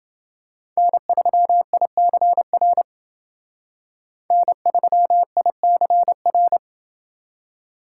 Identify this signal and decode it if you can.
Morse code, heard as a keyed tone, about 28 wpm, 700 Hz. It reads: N3ICR N3ICR